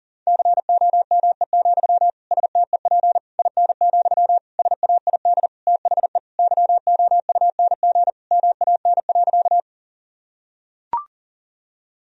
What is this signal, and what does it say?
Morse code, heard as a keyed tone, about 37 wpm, 700 Hz. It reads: COME, STEP IN, SAID THE YOUNG MAN.
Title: Sentences from Top 1000 Words - Review Part 2 - 40wpm